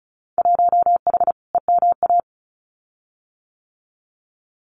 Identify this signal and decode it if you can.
Morse code, heard as a keyed tone, about 36 wpm, 700 Hz. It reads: 1H EMA
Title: Winter Field Day Exchanges Encore 35wpm